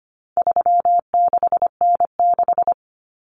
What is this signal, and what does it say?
Morse code, heard as a keyed tone, about 25 wpm, 700 Hz. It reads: 36N6
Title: Common Cut Numbers - 4-Digit 25wpm